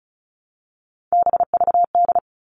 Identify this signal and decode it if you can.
Morse code, heard as a keyed tone, about 34 wpm, 700 Hz. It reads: BVD
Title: All Cut Numbers - 3-Digit 35wpm